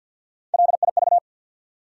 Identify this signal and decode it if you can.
Morse code, heard as a keyed tone, about 54 wpm, 700 Hz. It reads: LIV
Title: USA Common Names - Max 3 Letters - 50wpm